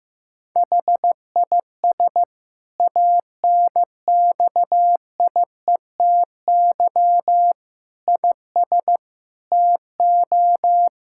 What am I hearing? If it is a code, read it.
Morse code, heard as a keyed tone, about 15 wpm, 700 Hz. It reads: HIS ANXIETY IS TO
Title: Sentences from Top 1600 Words 15wpm